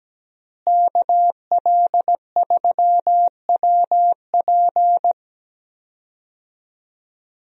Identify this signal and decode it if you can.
Morse code, heard as a keyed tone, about 17 wpm, 700 Hz. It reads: KL3WP
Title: USA Call Signs - 5 Characters - Repeated 2x - 17wpm